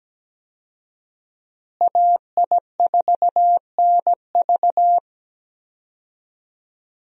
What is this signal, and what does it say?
Morse code, heard as a keyed tone, about 17 wpm, 700 Hz. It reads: AI4NV
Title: USA Call Signs - 5 Characters - Repeated 2x - 17wpm